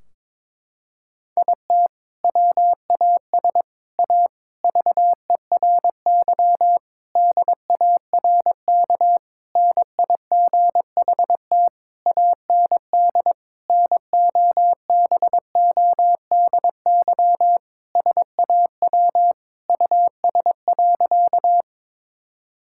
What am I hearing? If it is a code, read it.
Morse code, heard as a keyed tone, about 22 wpm, 700 Hz. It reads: IT WAS A VERY DARK NIGHT AND NOBODY SAW US.
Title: Sentences from Top 1600 Words 22wpm